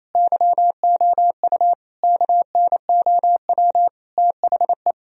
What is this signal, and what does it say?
Morse code, heard as a keyed tone, about 27 wpm, 700 Hz. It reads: YOU KNOW THE
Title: Sentences from Top 1000 Words - Review Part 2 - 28wpm